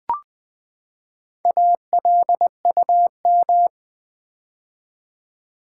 Morse code, heard as a keyed tone, about 20 wpm, 700 Hz. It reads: ALUM
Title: Build A Word - 4 Letters At A Time - 8-Letter Words - 20wpm